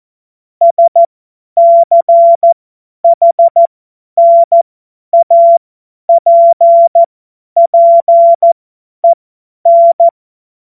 Morse code, heard as a keyed tone, about 14 wpm, spaced as for 10 wpm, 680 Hz. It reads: SCHNAPPEN
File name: Morsekurs / Witze / Witz 15. Februar 2022 50 F70